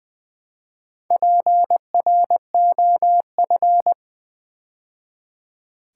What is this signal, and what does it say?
Morse code, heard as a keyed tone, about 20 wpm, 700 Hz. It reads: PROF
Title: Build A Word - 2 Letters At A Time - 8-Letter Words - 20wpm